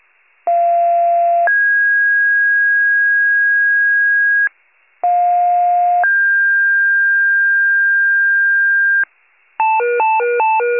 Two-tone page, heard as a one-second tone, then a three-second tone, about 690 then 1650 Hz, about 710 then 1600 Hz. It is heard from 0.5 to 9.0 s.
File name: pager / 41 RESCUE DUTY / wcalert